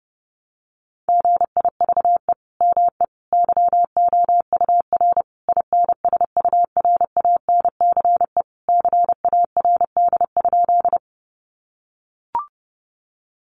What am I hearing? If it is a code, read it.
Morse code, heard as a keyed tone, about 30 wpm, 700 Hz. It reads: GIVE ME YOUR INSURANCE CARD?
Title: Sentences from Top 1100 Words 30wpm